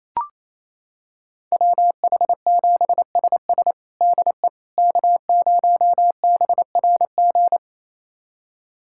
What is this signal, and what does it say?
Morse code, heard as a keyed tone, about 28 wpm, 700 Hz. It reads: WH7SS DE K0BRG
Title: USA Callsign DE Callsign - 5 Characters - 28wpm